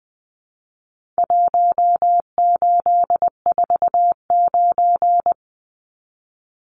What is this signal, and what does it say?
Morse code, heard as a keyed tone, about 20 wpm, 700 Hz. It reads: 1849
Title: Numbers - Contest Style - 4-Digit 20wpm